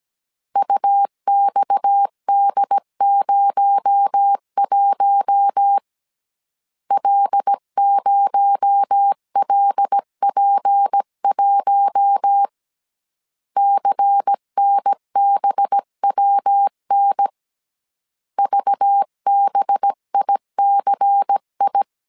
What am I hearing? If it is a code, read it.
Morse code, heard as a keyed tone, about 17 wpm, 800 Hz. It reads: UXD01 L0LP1 CNBWN VBICI
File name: CW / DET / 1 6 1500C S C 16wpm 2733